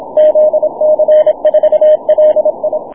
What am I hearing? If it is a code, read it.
Morse code, heard as a keyed tone, about 26 wpm, 610 Hz. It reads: ZC4LI
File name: wp / zc4li 6m cw